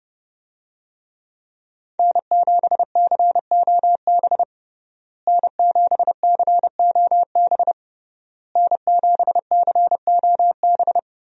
Morse code, heard as a keyed tone, about 30 wpm, 700 Hz. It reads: N7COB N7COB N7COB
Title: USA Call Signs - 5 Characters - Repeated 3x - 30wpm